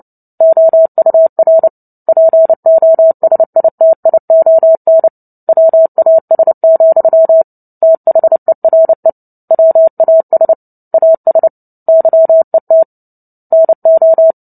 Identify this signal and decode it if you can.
Morse code, heard as a keyed tone, about 30 wpm, 650 Hz. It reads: OUR POSITION WAS, THERE WAS AS YET NO
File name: Book Two / 03-Chapter 3 - 29 WPM